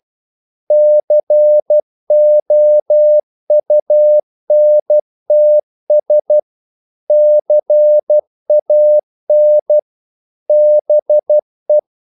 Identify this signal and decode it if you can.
Morse code, heard as a keyed tone, about 12 wpm, 600 Hz. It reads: COUNTS CAN BE
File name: CW / Practice / EnglishQuotations0112